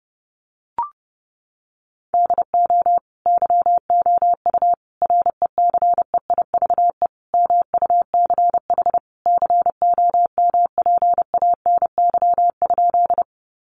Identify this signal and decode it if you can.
Morse code, heard as a keyed tone, about 30 wpm, 700 Hz. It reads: DO YOU RECEIVE MUCH COMPANY?
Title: Sentences from Top 1000 Words 30wpm